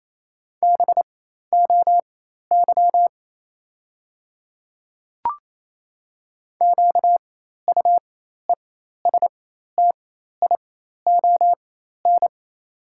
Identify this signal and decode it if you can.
Morse code, heard as a keyed tone, about 28 wpm, spaced as for 13 wpm, 700 Hz. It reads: BOY QUESTION
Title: 4x Character Spacing - Top 500 Words - 28wpm